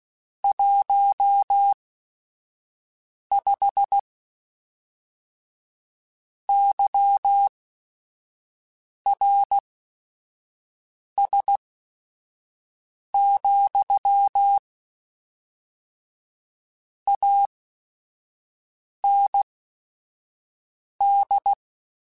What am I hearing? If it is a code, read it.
Morse code, heard as a keyed tone, about 16 wpm, spaced as for 5 wpm, 800 Hz. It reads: 15 YRS, AND
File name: QSOs / QSOs-mp3s / WD0BGP 1@5